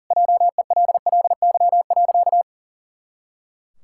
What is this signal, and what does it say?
Morse code, heard as a keyed tone, about 40 wpm, 700 Hz. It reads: JELLY.
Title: Pangram Challenge v2 - 40wpm